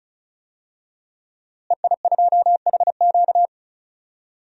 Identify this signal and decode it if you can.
Morse code, heard as a keyed tone, about 36 wpm, 700 Hz. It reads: EI2HQ
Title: DXCC-100 Call Signs - 5 Characters - Repeated 3x - 35wpm